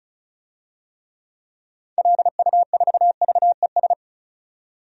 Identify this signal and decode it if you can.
Morse code, heard as a keyed tone, about 36 wpm, 700 Hz. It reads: LU4VES
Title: DXCC-100 Call Signs - 6 Characters - 35wpm